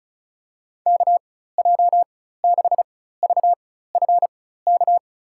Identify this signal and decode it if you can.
Morse code, heard as a keyed tone, about 36 wpm, spaced as for 16 wpm, 700 Hz. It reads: KJ6VFK
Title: USA Call Signs - 6 Characters - 4x Character Spacing - 35wpm